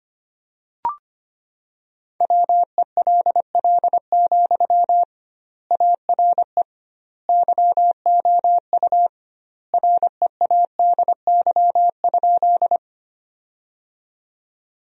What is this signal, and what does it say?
Morse code, heard as a keyed tone, about 25 wpm, 700 Hz. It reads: WELL, ARE YOU READY?
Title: 2x Word-Spacing - Sentences from Top 400 Words - 25wpm